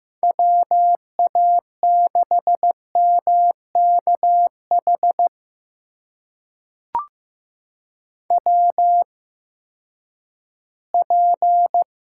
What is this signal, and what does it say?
Morse code, heard as a keyed tone, about 15 wpm, spaced as for 10 wpm, 700 Hz. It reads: WA6MKH WP
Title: USA Call Signs - 8x Character Spacing - 15wpm